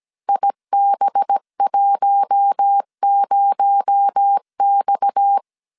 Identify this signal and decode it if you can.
Morse code, heard as a keyed tone, about 17 wpm, 800 Hz. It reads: IB10X
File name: CW / DET / 1 6 1500C S A 16wpm 2635